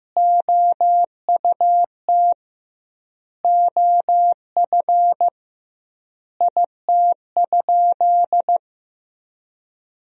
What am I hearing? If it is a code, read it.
Morse code, heard as a keyed tone, about 16 wpm, 700 Hz. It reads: OUT OF IT?
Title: 2x Word-Spacing - Sentences from Top 300 Words - 15wpm